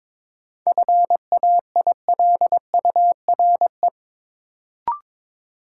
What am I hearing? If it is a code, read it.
Morse code, heard as a keyed tone, about 22 wpm, 700 Hz. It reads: FAILURE
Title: Sentences from Top 900 Words - New Word Drill - 22wpm